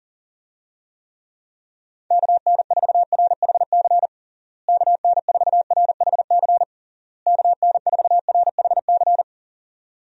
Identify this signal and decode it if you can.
Morse code, heard as a keyed tone, about 40 wpm, 700 Hz. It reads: KN4RHC KN4RHC KN4RHC
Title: USA Call Signs - 6 Characters - Repeated 3x - 40wpm